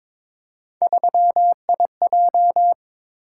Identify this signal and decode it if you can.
Morse code, heard as a keyed tone, about 22 wpm, 700 Hz. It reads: 3IJ
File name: practice / 22wpm / Three Letter-Number Combinations - 22wpm